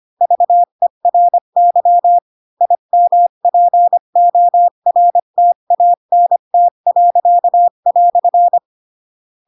Morse code, heard as a keyed tone, about 25 wpm, 700 Hz. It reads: VERY IMPORTANT."
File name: Pinocchio / Pinocchio ch8 20 25 ik1hge